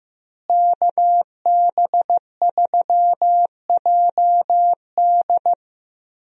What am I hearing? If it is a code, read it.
Morse code, heard as a keyed tone, about 15 wpm, 700 Hz. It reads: KB3JD
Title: USA Call Signs - 5 Characters - Repeated 3x - 15wpm